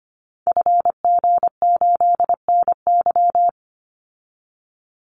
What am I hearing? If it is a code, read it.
Morse code, heard as a keyed tone, about 25 wpm, 700 Hz. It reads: FG8NY
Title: Call Signs Repeated 2x - 25wpm